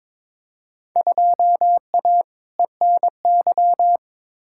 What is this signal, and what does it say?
Morse code, heard as a keyed tone, about 22 wpm, 700 Hz. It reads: 2A ENY
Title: Field Day Exchanges Encore - 22wpm